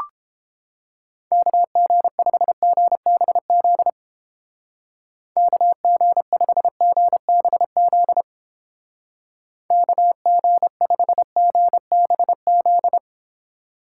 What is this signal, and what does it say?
Morse code, heard as a keyed tone, about 31 wpm, 700 Hz. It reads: KG5GBZ KG5GBZ KG5GBZ